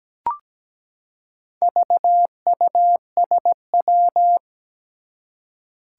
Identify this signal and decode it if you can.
Morse code, heard as a keyed tone, about 17 wpm, 700 Hz. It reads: VUSW
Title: Four Challenging Letter-Number Combinations 17wpm